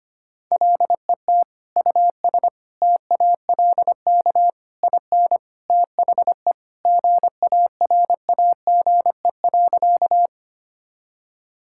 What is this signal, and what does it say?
Morse code, heard as a keyed tone, about 25 wpm, 700 Hz. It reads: LET US TALK IN THE GARAGE.
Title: Sentences from Top 1600 Words 25wpm